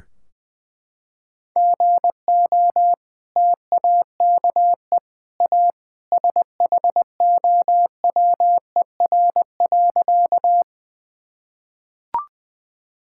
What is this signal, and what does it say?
Morse code, heard as a keyed tone, about 20 wpm, 700 Hz. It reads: GO TAKE A SHOWER.
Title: Sentences from Top 1600 Words 20wpm